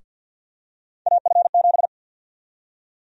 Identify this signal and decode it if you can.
Morse code, heard as a keyed tone, about 52 wpm, 700 Hz. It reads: AF7
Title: USA Call Sign Prefixes - Repeated 2x - 50wpm